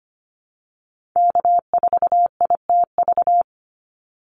Morse code, heard as a keyed tone, about 25 wpm, 700 Hz. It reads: K4ITV
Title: Call Signs Repeated 3x - 25wpm